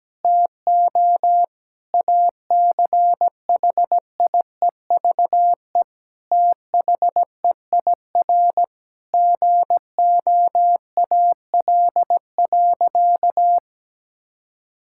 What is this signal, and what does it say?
Morse code, heard as a keyed tone, about 17 wpm, 700 Hz. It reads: TO ACHIEVE THEIR GOAL.
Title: Sentences from Top 1000 Words - Review Part 6 - 17wpm